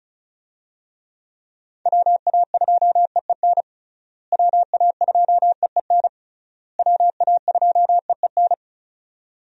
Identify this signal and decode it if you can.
Morse code, heard as a keyed tone, about 36 wpm, 700 Hz. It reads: WA2EEN WA2EEN WA2EEN
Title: USA Call Signs - 6 Characters - Repeated 3x - 35wpm